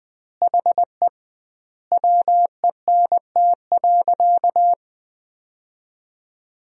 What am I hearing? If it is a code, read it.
Morse code, heard as a keyed tone, about 20 wpm, 700 Hz. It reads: HE WENT.
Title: 2x Word-Spacing - Sentences from Top 200 Words - 20wpm